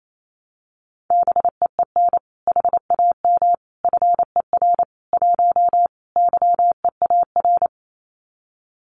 Morse code, heard as a keyed tone, about 28 wpm, 700 Hz. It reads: BEEN HAM FER 1 YEAR
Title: Common QSO Phrases Encore 28wpm